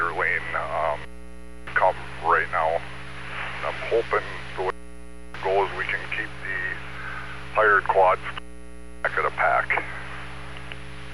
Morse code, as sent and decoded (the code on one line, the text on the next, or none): none